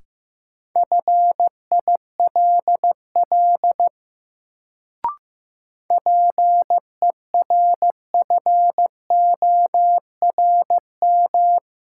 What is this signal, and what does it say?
Morse code, heard as a keyed tone, about 15 wpm, 700 Hz. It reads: FILL PERFORM
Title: Sentences from Top 1500 Words - New Word Drill Alt - 15wpm